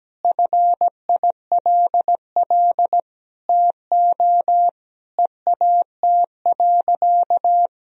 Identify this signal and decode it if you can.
Morse code, heard as a keyed tone, about 17 wpm, 700 Hz. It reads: FILL TO EAT.
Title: Sentences from Top 1000 Words - Review Part 3 - 17wpm